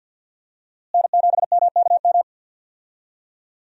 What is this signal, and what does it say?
Morse code, heard as a keyed tone, about 51 wpm, 700 Hz. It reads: N7MKM
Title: USA Call Signs - 5 Characters - Repeated 2x - 50wpm